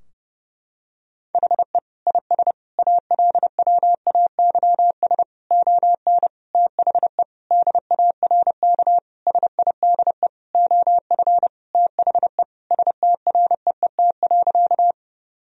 Morse code, heard as a keyed tone, about 32 wpm, 700 Hz. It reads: HE IS ALWAYS ON THE DARK SIDE OF THE STREET.
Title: Sentences from Top 1000 Words - Review Part 3 - 30wpm